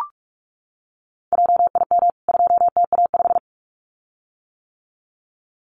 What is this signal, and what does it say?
Morse code, heard as a keyed tone, about 44 wpm, 700 Hz. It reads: JIM 2TA5
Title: CWT Exchange 45wpm